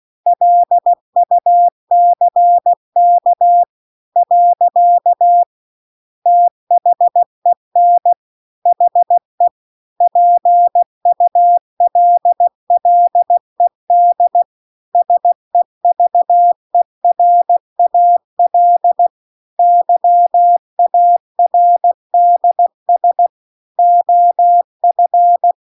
Morse code, heard as a keyed tone, about 16 wpm, 700 Hz. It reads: LUCK . THEN HE PULLED SEVERAL YARDS OF